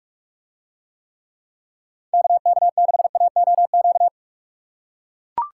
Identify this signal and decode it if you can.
Morse code, heard as a keyed tone, about 44 wpm, 700 Hz. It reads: KK6AOQ